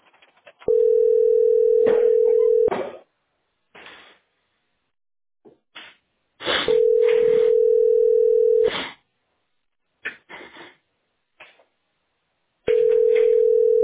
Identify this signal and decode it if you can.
Ringback tone (North America), 440 + 480 Hz.